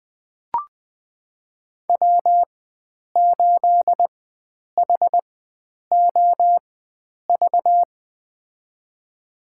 Morse code, heard as a keyed tone, about 20 wpm, spaced as for 9 wpm, 700 Hz. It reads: W8HOV
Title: USA Call Signs - 4x Character Spacing - 20wpm